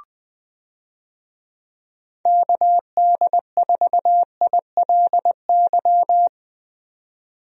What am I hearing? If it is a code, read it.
Morse code, heard as a keyed tone, about 20 wpm, 700 Hz. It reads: KD4ILY
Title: USA Call Signs - 6 Characters - 20wpm